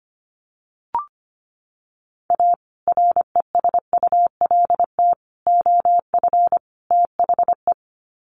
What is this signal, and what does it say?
Morse code, heard as a keyed tone, about 25 wpm, 700 Hz. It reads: A RESULT OF THE
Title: Sets of 4 Words 25wpm